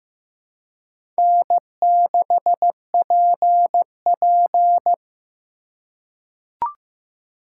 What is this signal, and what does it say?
Morse code, heard as a keyed tone, about 15 wpm, 700 Hz. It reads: N6PP
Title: USA Call Signs - 4x Character Spacing - 15wpm